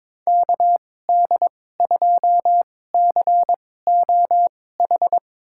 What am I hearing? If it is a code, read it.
Morse code, heard as a keyed tone, about 21 wpm, spaced as for 16 wpm, 700 Hz. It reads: KD2COH
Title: USA Call Signs - 6 Characters - 2x Character Spacing - 22wpm